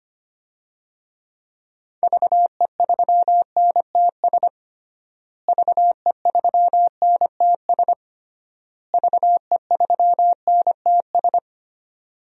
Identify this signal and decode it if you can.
Morse code, heard as a keyed tone, about 25 wpm, 700 Hz. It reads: VE3NTS VE3NTS VE3NTS
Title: DXCC-100 Call Signs - 6 Characters - Repeated 3x - 25wpm